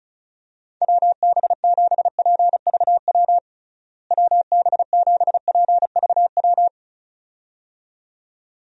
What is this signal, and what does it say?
Morse code, heard as a keyed tone, about 36 wpm, 700 Hz. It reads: WB7PVW WB7PVW
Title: USA Call Signs - 6 Characters - Repeated 2x - 35wpm